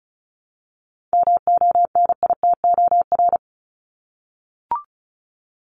Morse code, heard as a keyed tone, about 34 wpm, 700 Hz. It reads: MONITOR
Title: Top 1751-2000 Words 35wpm